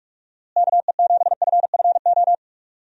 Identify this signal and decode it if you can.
Morse code, heard as a keyed tone, about 46 wpm, 700 Hz. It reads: KE7FFO
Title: USA Call Signs - 6 Characters - Repeated 3x - 45wpm